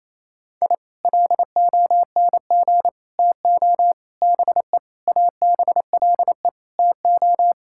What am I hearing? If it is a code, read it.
Morse code, heard as a keyed tone, about 28 wpm, 700 Hz. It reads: I LONG TO BE ABLE TO
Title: Sentences from Top 1000 Words - Review Part 6 - 28wpm